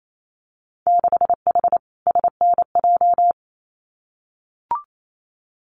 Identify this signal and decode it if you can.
Morse code, heard as a keyed tone, about 28 wpm, 700 Hz. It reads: 6H SNJ
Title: Winter Field Day Exchanges Encore 28wpm